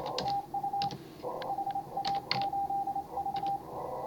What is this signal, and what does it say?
Morse code, heard as a keyed tone, about 26 wpm, 810 Hz, with radio noise. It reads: TU HA8V
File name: CW / HA5MA RHOB HANG CW 2017